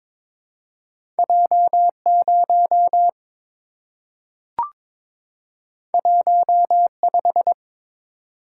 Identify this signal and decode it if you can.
Morse code, heard as a keyed tone, about 22 wpm, 700 Hz. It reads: J0 15
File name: practice / 22wpm / Two Letter-Number Combinations - 22wpm